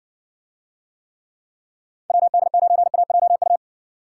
Morse code, heard as a keyed tone, about 66 wpm, 700 Hz. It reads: WD9AJU